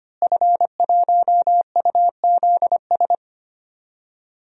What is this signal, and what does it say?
Morse code, heard as a keyed tone, about 25 wpm, 700 Hz. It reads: F1UZS
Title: DXCC-100 Call Signs - 5 Characters - Repeated 3x - 25wpm